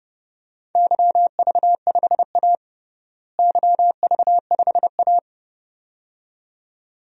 Morse code, heard as a keyed tone, about 32 wpm, 700 Hz. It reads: YV5A YV5A
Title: DXCC-100 Call Signs - 4 Characters - Repeated 2x - 30wpm